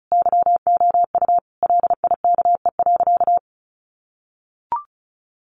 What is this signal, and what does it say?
Morse code, heard as a keyed tone, about 34 wpm, 700 Hz. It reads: YOU LIKE.
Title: Sentences from Top 1500 Words 35wpm